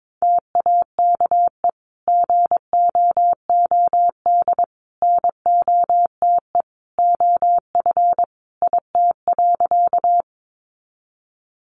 Morse code, heard as a keyed tone, about 22 wpm, 700 Hz. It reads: TAKE GOOD NOTE OF IT.
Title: Sentences from Top 500 Words 22wpm